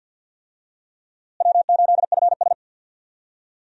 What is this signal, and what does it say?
Morse code, heard as a keyed tone, about 49 wpm, 700 Hz. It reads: W8FS
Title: USA Call Signs - 4 Characters - Repeated 3x - 50wpm